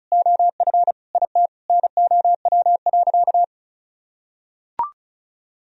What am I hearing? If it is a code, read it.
Morse code, heard as a keyed tone, about 34 wpm, 700 Hz. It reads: OF IT NOW.
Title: Sentences from Top 1000 Words - Review Part 2 - 35wpm